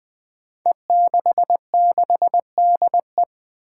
Morse code, heard as a keyed tone, about 20 wpm, 700 Hz. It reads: E66DE
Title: Five Challenging Letter-Number Combinations 20wpm